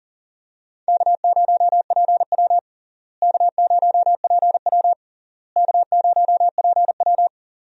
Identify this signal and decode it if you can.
Morse code, heard as a keyed tone, about 37 wpm, 700 Hz. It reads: K0PW K0PW K0PW
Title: USA Call Signs - 4 Characters - Repeated 3x - 40wpm